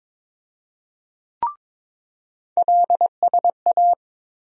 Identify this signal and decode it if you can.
Morse code, heard as a keyed tone, about 22 wpm, 700 Hz. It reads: LSA
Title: USA Call Sign Postfixes - 3 Letters 22wpm